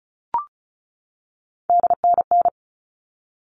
Morse code, heard as a keyed tone, about 34 wpm, 700 Hz. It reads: DNN
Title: All Cut Numbers - 3-Digit 35wpm